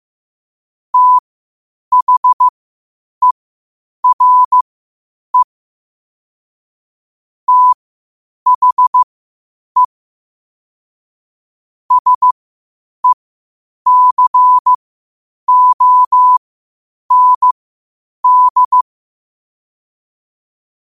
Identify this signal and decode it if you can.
Morse code, heard as a keyed tone, about 15 wpm, spaced as for 8 wpm, 1000 Hz. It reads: THERE THE SECOND
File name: qotdfiles / 8wpm